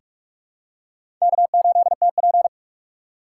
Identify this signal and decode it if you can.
Morse code, heard as a keyed tone, about 46 wpm, 700 Hz. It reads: K8TP
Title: USA Call Signs - 3x Character Spacing - 45wpm